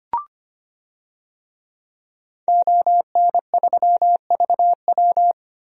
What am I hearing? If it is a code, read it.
Morse code, heard as a keyed tone, about 25 wpm, 700 Hz. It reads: ON3VW
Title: DXCC-100 Call Signs - 5 Characters - 25wpm